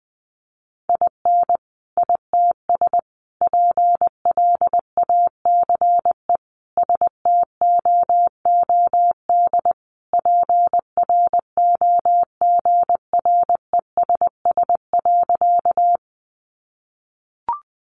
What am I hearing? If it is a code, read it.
Morse code, heard as a keyed tone, about 20 wpm, 700 Hz. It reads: IN ITS PLACE STOOD PROGRESS.
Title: Sentences from Top 1400 Words 20wpm